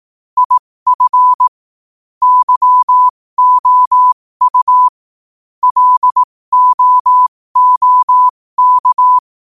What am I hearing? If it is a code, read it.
Morse code, heard as a keyed tone, about 18 wpm, spaced as for 15 wpm, 1000 Hz. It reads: IF YOU LOOK
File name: qotdfiles / 15wpm